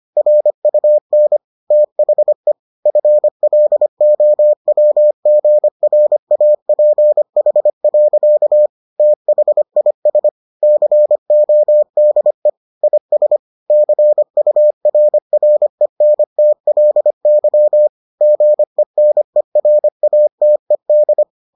Morse code, heard as a keyed tone, about 25 wpm, 600 Hz. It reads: RUN THE FLOWGRAPH. THIS CODE IS CURRENTLY GENERATED